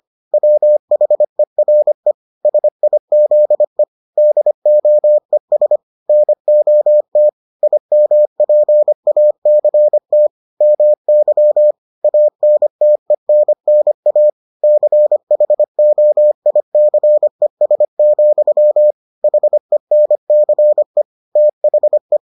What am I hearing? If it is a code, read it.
Morse code, heard as a keyed tone, about 25 wpm, 600 Hz. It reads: WHERE SIZE DOES NOT IMPACT MY ANTENNA CHOICES, HENCE THE